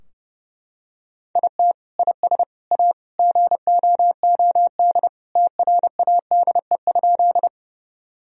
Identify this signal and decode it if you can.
Morse code, heard as a keyed tone, about 30 wpm, 700 Hz. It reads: IT IS A GOOD TRADE?
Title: Sentences from Top 1000 Words - Review Part 4 - 30wpm